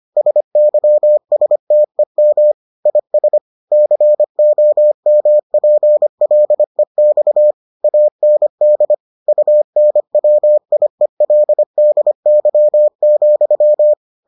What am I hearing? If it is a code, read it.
Morse code, heard as a keyed tone, about 25 wpm, 600 Hz. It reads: SYSTEM IS COMPLEX AND UNWIELDY,